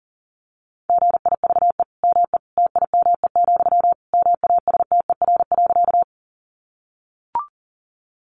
Morse code, heard as a keyed tone, about 40 wpm, 700 Hz. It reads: GIVE ME TIME, MASTER.
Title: Sentences from Top 1300 Words 40wpm